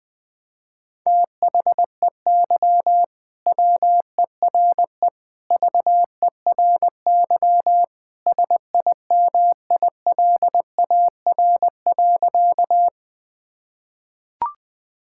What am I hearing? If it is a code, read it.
Morse code, heard as a keyed tone, about 20 wpm, 700 Hz. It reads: THEY WERE VERY SIMILAR.
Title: Sentences from Top 1000 Words - Review Part 5 - 20wpm